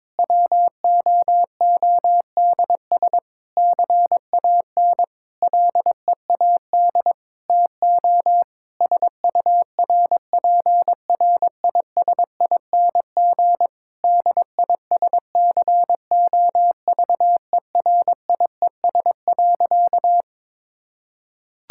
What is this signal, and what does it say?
Morse code, heard as a keyed tone, about 22 wpm, 700 Hz. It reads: WOODS CAN LEAD TO SURPRISING DISCOVERIES.
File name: Difficult Sentences with Hints - 22wpm